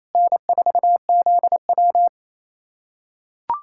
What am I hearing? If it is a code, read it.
Morse code, heard as a keyed tone, about 28 wpm, 700 Hz. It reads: N4ZW
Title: USA Callsign DE Callsign - 4 Characters - 28wpm